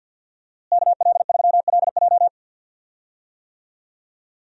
Morse code, heard as a keyed tone, about 51 wpm, 700 Hz. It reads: KL3LJ
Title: USA Call Signs - 5 Characters - Repeated 2x - 50wpm